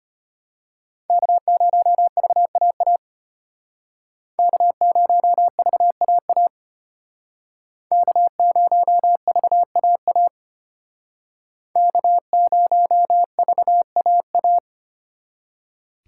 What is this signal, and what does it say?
Morse code, heard as a keyed tone, about 32 wpm, 700 Hz. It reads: K0VAA K0VAA K0VAA K0VAA